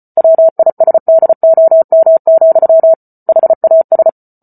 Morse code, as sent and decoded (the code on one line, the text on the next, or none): .-- .. ... -.. --- -- --..-- / .... .- ...
WISDOM, HAS